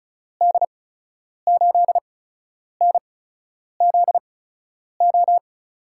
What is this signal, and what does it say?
Morse code, heard as a keyed tone, about 34 wpm, spaced as for 10 wpm, 700 Hz. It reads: D8NZO
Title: USA Call Signs - 6 Characters - 8x Character Spacing - 35wpm